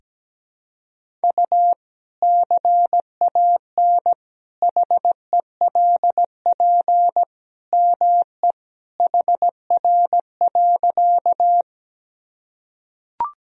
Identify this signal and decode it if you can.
Morse code, heard as a keyed tone, about 17 wpm, 700 Hz. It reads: U CAN HELP ME HR.
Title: Sentences from Top 200 Words and Abbreviations 17wpm